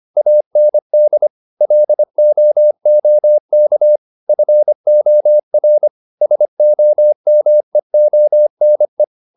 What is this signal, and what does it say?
Morse code, heard as a keyed tone, about 25 wpm, 600 Hz. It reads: AND LOOK FOR SOMEONE